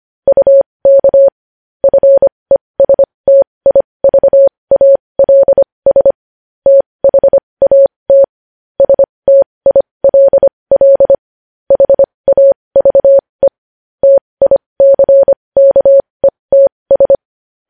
Morse code, heard as a keyed tone, about 25 wpm, spaced as for 20 wpm, 550 Hz. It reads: UK FESTIVALS THAT STILL HAVE TICKETS
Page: News headlines in Morse code at 20 WPM